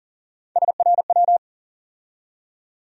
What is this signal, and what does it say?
Morse code, heard as a keyed tone, about 40 wpm, 700 Hz. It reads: SRW
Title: USA Call Sign Postfixes - 3 Letters 40wpm